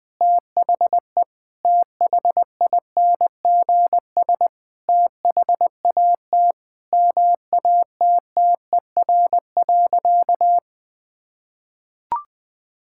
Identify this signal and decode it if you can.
Morse code, heard as a keyed tone, about 20 wpm, 700 Hz. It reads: THE THINGS THAT MATTER.
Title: Sentences with 5-Word Phrases - 20wpm